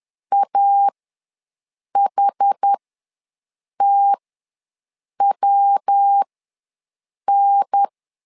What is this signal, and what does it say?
Morse code, heard as a keyed tone, about 10 wpm, spaced as for 6 wpm, 800 Hz. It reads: AHTWN
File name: LEZIONI / 3 MWRO / 3 MW ws 10wpm 2213